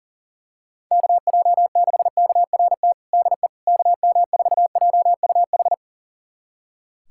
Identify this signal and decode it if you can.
Morse code, heard as a keyed tone, about 40 wpm, 700 Hz. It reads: KJ6KRT DE KM4JUH
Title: USA Callsign DE Callsign 40wpm